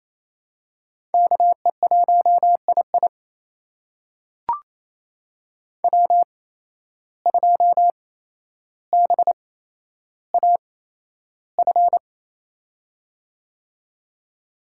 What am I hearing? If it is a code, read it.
Morse code, heard as a keyed tone, about 28 wpm, spaced as for 12 wpm, 700 Hz. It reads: KE1II W2BAF
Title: USA Call Signs - 5 Characters - 8x Character Spacing - 28wpm